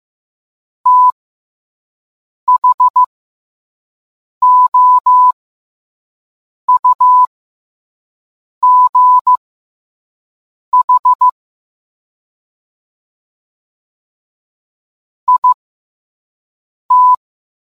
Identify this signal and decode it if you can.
Morse code, heard as a keyed tone, about 15 wpm, spaced as for 5 wpm, 1000 Hz. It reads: THOUGH IT